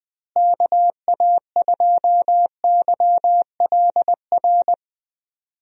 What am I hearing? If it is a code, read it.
Morse code, heard as a keyed tone, about 20 wpm, 700 Hz. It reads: KA2YLR
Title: USA Call Signs - 6 Characters - Repeated 3x - 20wpm